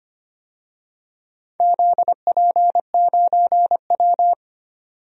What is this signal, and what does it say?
Morse code, heard as a keyed tone, about 25 wpm, 700 Hz. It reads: ZP9W
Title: DXCC-100 Call Signs - 4 Characters - 25wpm